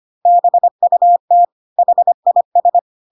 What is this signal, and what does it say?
Morse code, heard as a keyed tone, about 25 wpm, 700 Hz. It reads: BUT HIS
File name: Pinocchio / Pinocchio ch4 25 25 ik1hge